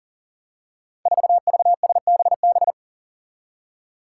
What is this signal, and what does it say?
Morse code, heard as a keyed tone, about 40 wpm, 700 Hz. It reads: 4VSBB
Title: Five Challenging Letter-Number Combinations 40wpm